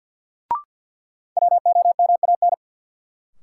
Morse code, heard as a keyed tone, about 53 wpm, 700 Hz. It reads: WOMAN
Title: Sentences from Top 1000 Words - Review Part 2 - New Word Drill - 50wpm